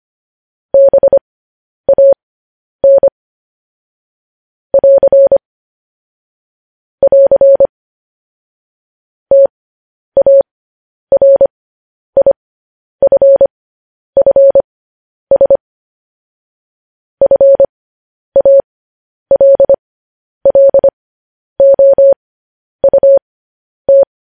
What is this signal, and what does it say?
Morse code, heard as a keyed tone, about 25 wpm, spaced as for 10 wpm, 550 Hz. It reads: BAN <AR> <AR> TARIFFS FALLOUT
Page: news-headlines-in-morse-code-at-10-wpm